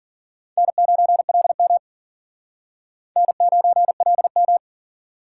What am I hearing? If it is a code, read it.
Morse code, heard as a keyed tone, about 50 wpm, 700 Hz. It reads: N9LM N9LM